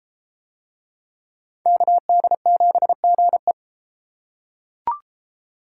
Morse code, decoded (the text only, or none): KD7GE